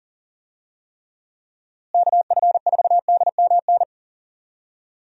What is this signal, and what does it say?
Morse code, heard as a keyed tone, about 40 wpm, 700 Hz. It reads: KF4DMN
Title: USA Call Signs - 6 Characters - Repeated 3x - 40wpm